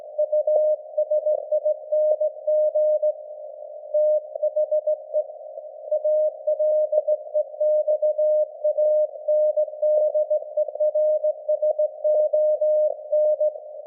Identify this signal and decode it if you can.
Morse code, heard as a keyed tone, about 18 wpm, 610 Hz. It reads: USING THE ALEXANDERSON
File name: laengstwellenempfang / SDRSharp 17kHz AF